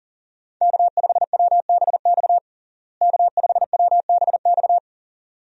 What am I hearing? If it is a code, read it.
Morse code, heard as a keyed tone, about 40 wpm, 700 Hz. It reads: K5WBX K5WBX
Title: USA Call Signs - 5 Characters - Repeated 2x - 40wpm